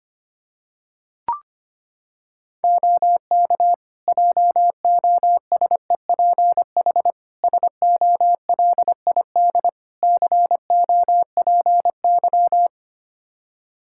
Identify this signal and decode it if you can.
Morse code, heard as a keyed tone, about 25 wpm, 700 Hz. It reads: OK JOSEPH SOLID COPY
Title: Practice QSO - Exchange 3 - More Info - 25wpm